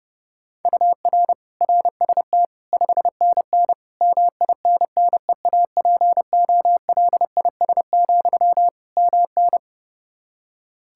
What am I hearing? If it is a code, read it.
Morse code, heard as a keyed tone, about 32 wpm, 700 Hz. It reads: UR RST 5NN MINNEAPOLIS, MN
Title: Practice QSO - Exchange 1 - Beginning - 30wpm